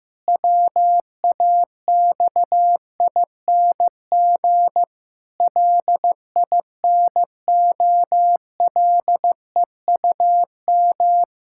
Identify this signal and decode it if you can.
Morse code, heard as a keyed tone, about 15 wpm, 700 Hz. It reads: WAXING LINOLEUM
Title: Pangram Challenge v2 - 15wpm